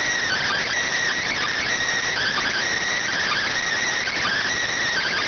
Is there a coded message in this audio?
Yes: Emergency Alert System (SAME) tones from 0.2 to 4.4 s, heard as harsh buzzing bursts.